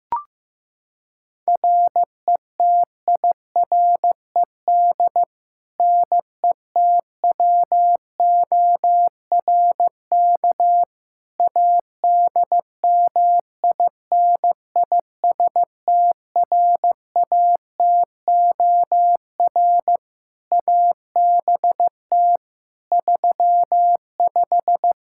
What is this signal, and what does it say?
Morse code, heard as a keyed tone, about 15 wpm, 700 Hz. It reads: RETIRED NETWORK ADMINISTRATOR ABT 35
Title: Practice QSO - Exchange 3 - More Info - 15wpm